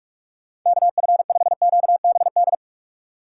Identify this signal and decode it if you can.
Morse code, heard as a keyed tone, about 44 wpm, 700 Hz. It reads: KF5QBD
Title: USA Call Signs - 6 Characters - 2x Character Spacing - 45wpm